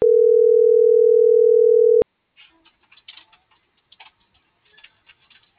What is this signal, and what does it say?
Ringback tone (North America), 440 + 480 Hz.